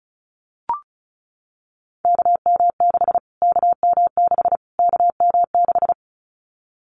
Morse code, heard as a keyed tone, about 34 wpm, 700 Hz. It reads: KM6 KM6 KM6
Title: Callsign Prefixes for Top 25 DXCC - Repeated 3x - 35wpm